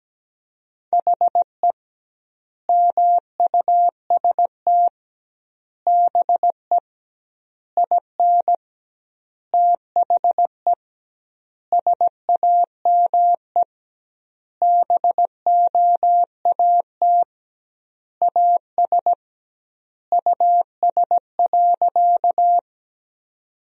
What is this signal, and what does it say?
Morse code, heard as a keyed tone, about 17 wpm, 700 Hz. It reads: HE MUST BE IN THE SAME BOAT AS US.
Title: 2x Word-Spacing - Sentences from Top 500 Words - 17wpm